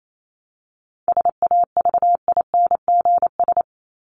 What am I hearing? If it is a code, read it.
Morse code, heard as a keyed tone, about 28 wpm, 700 Hz. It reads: SAVINGS
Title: Top 1001-1250 Words 28wpm